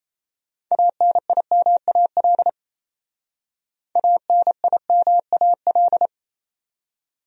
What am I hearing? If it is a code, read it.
Morse code, heard as a keyed tone, about 34 wpm, 700 Hz. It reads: ANIMAL ANIMAL
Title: Speed-Racing Top 200 Words 28wpm